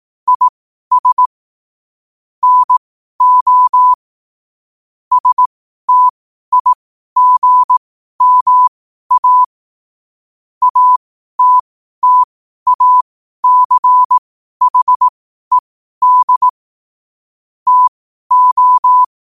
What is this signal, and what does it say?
Morse code, heard as a keyed tone, about 18 wpm, spaced as for 12 wpm, 1000 Hz. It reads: IS NO STIGMA ATTACHED TO
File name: qotdfiles / 12wpm